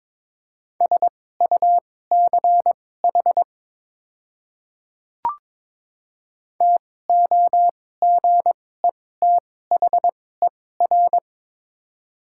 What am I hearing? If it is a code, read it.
Morse code, heard as a keyed tone, about 22 wpm, spaced as for 16 wpm, 700 Hz. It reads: SUCH TOGETHER
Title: 2x Character Spacing - Top 300 Words - 22wpm